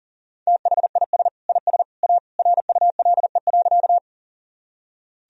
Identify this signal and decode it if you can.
Morse code, heard as a keyed tone, about 40 wpm, 700 Hz. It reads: THIS IS A RULE.
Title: Sentences from Top 1000 Words - Review Part 2 - 40wpm